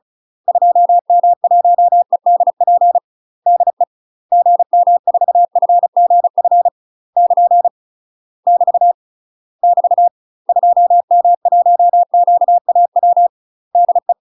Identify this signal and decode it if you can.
Morse code, heard as a keyed tone, about 34 wpm, 700 Hz. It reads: 2M1EDP DE GM4FGF <KN> <BT> <BT> 2M1QAW DE
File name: UK MASTER QSOs TRAINING / 35WPM / UK MASTER QSOs 0006